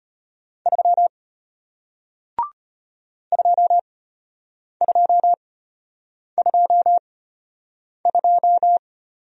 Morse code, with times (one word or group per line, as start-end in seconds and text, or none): 0.7-1.1 3
3.3-8.8 2222